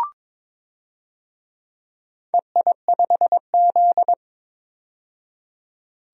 Morse code, heard as a keyed tone, about 23 wpm, 700 Hz. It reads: EI5Z
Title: DXCC-100 Call Signs - 4 Characters - 22wpm